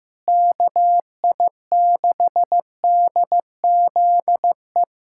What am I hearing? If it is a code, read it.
Morse code, heard as a keyed tone, about 15 wpm, 700 Hz. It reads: KI6DZE
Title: USA Callsign DE Callsign 15wpm